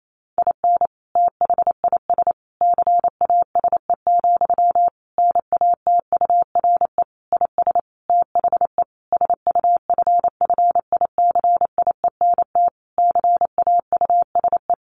Morse code, heard as a keyed tone, about 28 wpm, 700 Hz. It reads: IN THIS CASE, NATURE IS THE SUFFICIENT CAUSE
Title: Sentences from Top 1400 Words 28wpm